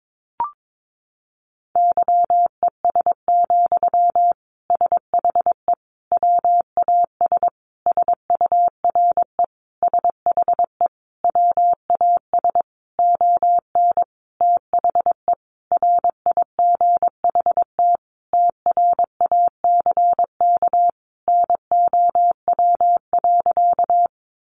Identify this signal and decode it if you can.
Morse code, heard as a keyed tone, about 22 wpm, 700 Hz. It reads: YES, SHE WAS SURE SHE WAS ON THE RIGHT TRACK NOW.
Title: Sentences from Top 1100 Words 22wpm